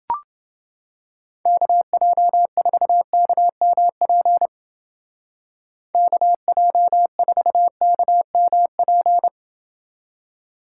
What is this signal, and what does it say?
Morse code, heard as a keyed tone, about 31 wpm, 700 Hz. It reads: KJ4KMP KJ4KMP